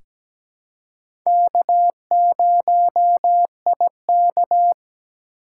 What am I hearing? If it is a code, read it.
Morse code, heard as a keyed tone, about 17 wpm, 700 Hz. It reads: K0IK
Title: USA Call Signs - 4 Characters - 2x Character Spacing - 17wpm